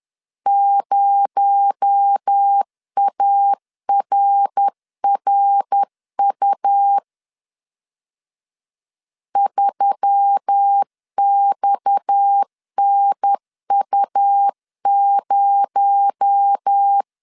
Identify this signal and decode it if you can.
Morse code, heard as a keyed tone, about 10 wpm, 800 Hz. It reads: 0ARRU 3XNU0
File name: CW / DET / 1 6 1500C S B 10wpm 4137